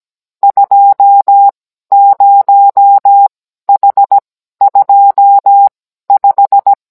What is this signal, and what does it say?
Morse code, heard as a keyed tone, about 17 wpm, spaced as for 12 wpm, 800 Hz. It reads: 20H25
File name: Cours-NR18-16mm